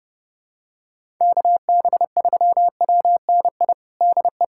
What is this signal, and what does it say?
Morse code, heard as a keyed tone, about 30 wpm, 700 Hz. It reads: KB3WNI DE
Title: USA Callsign DE Callsign 30wpm